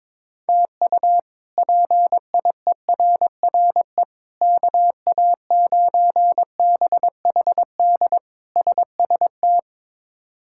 Morse code, heard as a keyed tone, about 22 wpm, 700 Hz. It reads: TU PIERRE KA9BHD SST
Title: K1USN SST QSO Practice - 22wpm